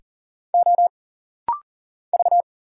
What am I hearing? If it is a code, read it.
Morse code, heard as a keyed tone, about 40 wpm, 700 Hz. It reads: O V
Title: Single Letters - Rapid-Fire 40wpm